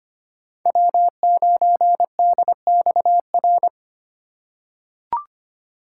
Morse code, heard as a keyed tone, about 25 wpm, 700 Hz. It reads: W9DXR
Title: K1USN SST QSO Practice - 25wpm